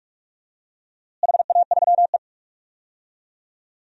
Morse code, heard as a keyed tone, about 46 wpm, 700 Hz. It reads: HA3E
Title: DXCC-100 Call Signs - 4 Characters - Repeated 2x - 45wpm